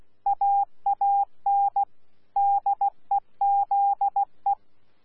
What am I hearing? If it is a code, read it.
Morse code, heard as a keyed tone, about 16 wpm, 800 Hz. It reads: AAN DEZE
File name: Z PI4VRZ / PI4VRZ 14-jun-2025